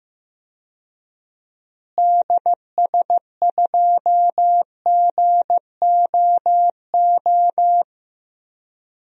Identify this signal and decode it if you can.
Morse code, heard as a keyed tone, about 15 wpm, 700 Hz. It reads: DS2GOO
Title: DXCC-100 Call Signs - 6 Characters - Repeated 3x - 15wpm